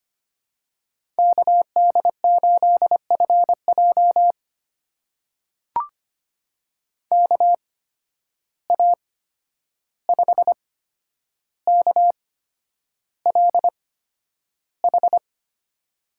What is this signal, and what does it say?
Morse code, heard as a keyed tone, about 25 wpm, spaced as for 10 wpm, 700 Hz. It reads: KD8FJ KA5KLH
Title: USA Call Signs - 8x Character Spacing - 25wpm